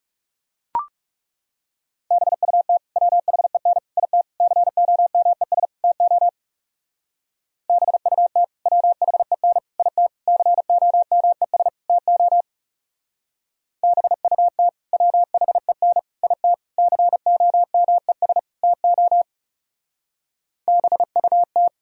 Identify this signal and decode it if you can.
Morse code, heard as a keyed tone, about 38 wpm, 700 Hz. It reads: BUT WHEN IT COMES TO BUT WHEN IT COMES TO BUT WHEN IT COMES TO BUT